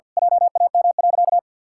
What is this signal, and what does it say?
Morse code, heard as a keyed tone, about 50 wpm, 700 Hz. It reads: JAM.
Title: Pangram Challenge v2 - 50wpm